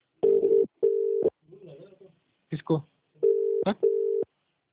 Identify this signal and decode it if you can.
Ringback tone (UK), 400 + 450 Hz, with speech.